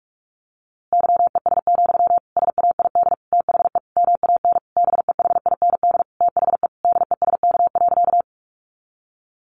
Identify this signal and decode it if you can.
Morse code, heard as a keyed tone, about 44 wpm, 700 Hz. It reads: YES, SAID THE MAN BEHIND THE DESK.
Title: A Sound Of Thunder 45wpm